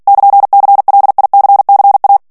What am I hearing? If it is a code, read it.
Morse code, heard as a keyed tone, about 48 wpm, 800 Hz. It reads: <KN>CLICCA
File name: morse mp3 / ITALIEN texte clair 1 vitesse 45 mots minute